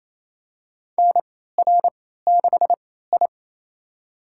Morse code, heard as a keyed tone, about 28 wpm, spaced as for 16 wpm, 700 Hz. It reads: NR6I
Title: USA Call Signs - 4 Characters - 3x Character Spacing - 28wpm